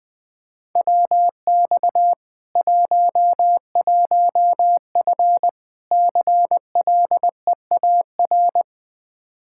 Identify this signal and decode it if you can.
Morse code, heard as a keyed tone, about 19 wpm, 700 Hz. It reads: WX 11F CLEAR
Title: Practice QSO - Exchange 2 - Info - 20wpm